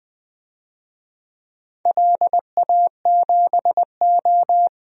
Morse code, heard as a keyed tone, about 20 wpm, 700 Hz. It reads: LA7O
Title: DXCC-100 Call Signs - 4 Characters - 20wpm